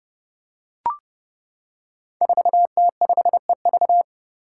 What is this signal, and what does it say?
Morse code, heard as a keyed tone, about 30 wpm, 700 Hz. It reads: 4T5EV
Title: Five Challenging Letter-Number Combinations 30wpm